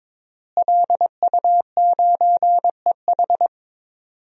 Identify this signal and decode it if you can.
Morse code, heard as a keyed tone, about 22 wpm, 700 Hz. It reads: LU9EH
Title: DXCC-100 Call Signs - 5 Characters - Repeated 3x - 22wpm